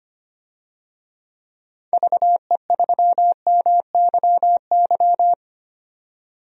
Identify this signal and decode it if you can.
Morse code, heard as a keyed tone, about 25 wpm, 700 Hz. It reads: VE3MYY
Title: DXCC-100 Call Signs - 6 Characters - 25wpm